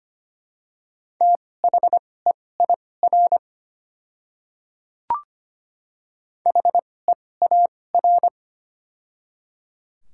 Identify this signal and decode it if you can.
Morse code, heard as a keyed tone, about 25 wpm, spaced as for 18 wpm, 700 Hz. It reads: THEIR HEAR
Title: 2x Character Spacing - Top 300 Words - 25wpm